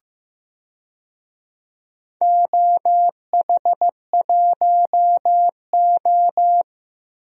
Morse code, heard as a keyed tone, about 15 wpm, 700 Hz. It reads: OH1O
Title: DXCC-100 Call Signs - 4 Characters - Repeated 2x - 15wpm